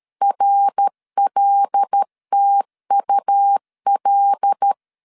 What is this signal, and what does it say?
Morse code, heard as a keyed tone, about 13 wpm, 800 Hz. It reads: RLTUL
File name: CW / DET / 1 6 1500C S A 12wpm 3440